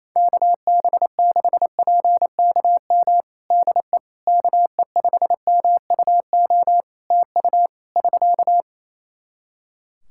Morse code, heard as a keyed tone, about 27 wpm, 700 Hz. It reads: KB6PKM DE KE5MUO TU <SK>
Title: Practice QSO - Exchange 4 - Ending - 28wpm